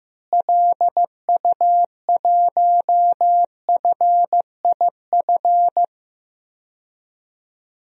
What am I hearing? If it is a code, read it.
Morse code, heard as a keyed tone, about 15 wpm, 700 Hz. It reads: LU1FIF
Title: DXCC-100 Call Signs - 6 Characters - Repeated 2x - 15wpm